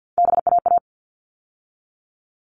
Morse code, heard as a keyed tone, about 50 wpm, 700 Hz. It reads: BAA
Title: All Cut Numbers - 3-Digit 50wpm